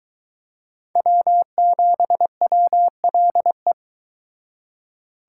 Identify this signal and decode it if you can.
Morse code, heard as a keyed tone, about 24 wpm, 700 Hz. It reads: W7WLE